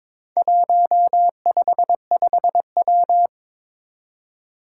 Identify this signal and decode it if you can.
Morse code, heard as a keyed tone, about 22 wpm, 700 Hz. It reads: 155W
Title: Four Challenging Letter-Number Combinations 22wpm